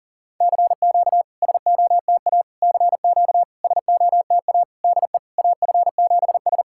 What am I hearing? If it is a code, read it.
Morse code, heard as a keyed tone, about 40 wpm, 690 Hz. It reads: CQ SOTA CQ SOTA DE AF7S
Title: SOTA QSO Practice 40wpm